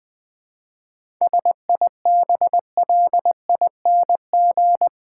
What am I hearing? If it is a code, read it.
Morse code, heard as a keyed tone, about 20 wpm, 700 Hz. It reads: SIBLING
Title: 7 Letter Words - First 3 Letters Given - 20wpm